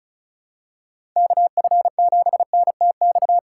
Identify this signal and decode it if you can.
Morse code, heard as a keyed tone, about 36 wpm, 700 Hz. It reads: KF7NTX
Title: USA Call Signs - 3x Character Spacing - 35wpm